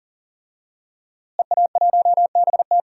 Morse code, heard as a keyed tone, about 40 wpm, 700 Hz. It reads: EA1BT
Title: DXCC-100 Call Signs - 5 Characters - Repeated 2x - 40wpm